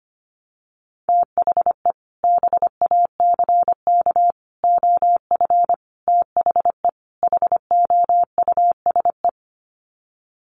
Morse code, heard as a keyed tone, about 25 wpm, 700 Hz. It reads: THE BACK OF THE HOUSE
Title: Sets of 5 Words Encore2 25wpm